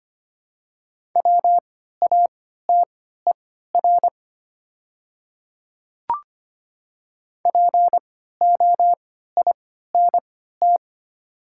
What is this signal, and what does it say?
Morse code, heard as a keyed tone, about 25 wpm, spaced as for 14 wpm, 700 Hz. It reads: WATER POINT
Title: 3x Character Spacing - Top 200 Words - 25wpm